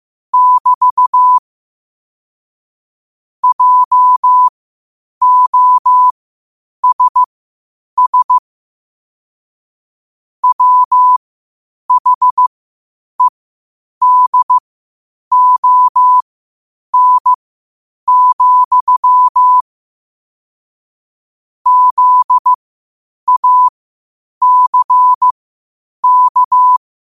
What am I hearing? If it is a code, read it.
Morse code, heard as a keyed tone, about 15 wpm, spaced as for 8 wpm, 1000 Hz. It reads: <BT> JOSS WHEDON, ZACK
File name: qotdfiles / 8wpm